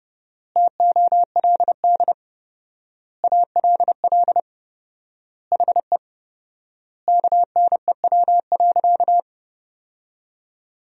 Morse code, heard as a keyed tone, about 32 wpm, 700 Hz. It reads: TOLD ALL HE KNEW.
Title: 4x Word-Spacing - Sentences from Top 400 Words - 30wpm